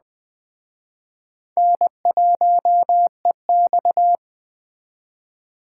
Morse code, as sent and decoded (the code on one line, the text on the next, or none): -. .---- . -..-
N1EX